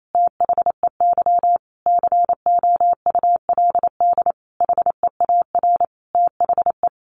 Morse code, heard as a keyed tone, about 28 wpm, 700 Hz. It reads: THEY COULD HEAR THE
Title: Sentences from Top 1100 Words 28wpm